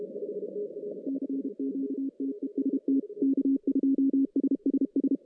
Morse code, heard as a keyed tone, about 32 wpm, 280 Hz. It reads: CQTEST K2SSS
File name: 2015 cq ww cw / 2015 cqwww k5tr - 0-41